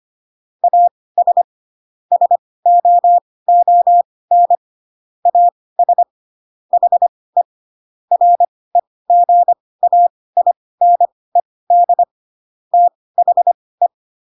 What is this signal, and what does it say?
Morse code, heard as a keyed tone, about 25 wpm, spaced as for 18 wpm, 700 Hz. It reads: AS SOON AS HE REGAINED THE
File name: Pinocchio / Pinocchio ch1 12 25 ik1hge